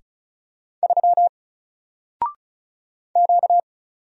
Morse code, heard as a keyed tone, about 35 wpm, 700 Hz. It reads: 3 Q
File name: Single-Letter-Number-35wpm